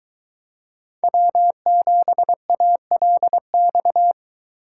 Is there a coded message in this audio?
Yes: Morse code from 1.0 to 4.1 s, heard as a keyed tone, about 23 wpm, 700 Hz. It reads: W7ALX